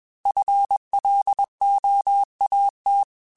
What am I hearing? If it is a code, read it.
Morse code, heard as a keyed tone, about 21 wpm, 800 Hz. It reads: FLOAT
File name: FL / FL mp3s / Fast FL @20